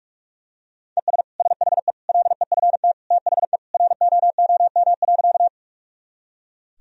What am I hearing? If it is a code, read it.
Morse code, heard as a keyed tone, about 46 wpm, 700 Hz. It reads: ES SHE LEFT THE ROOM.
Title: Sentences from Top 300 Words and Abbreviations 45wpm